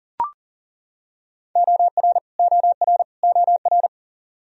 Morse code, heard as a keyed tone, about 40 wpm, 700 Hz. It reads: OR OR OR
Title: Top 100 Words Repeated 40wpm